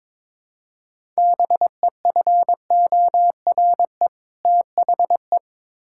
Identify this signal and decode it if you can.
Morse code, heard as a keyed tone, about 22 wpm, 700 Hz. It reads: BEFORE THE
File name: Build A Phrase - 2-Word Phrases - 22wpm